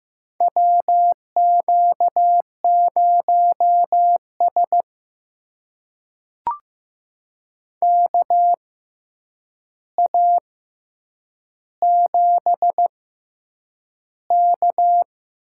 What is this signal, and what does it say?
Morse code, heard as a keyed tone, about 15 wpm, spaced as for 8 wpm, 700 Hz. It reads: WQ0S KA7K
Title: USA Call Signs - 4 Characters - 6x Character Spacing - 15wpm